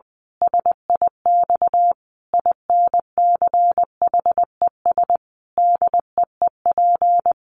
Morse code, heard as a keyed tone, about 20 wpm, 700 Hz. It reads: SIX INCHES DEEP
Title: A Sound Of Thunder 20wpm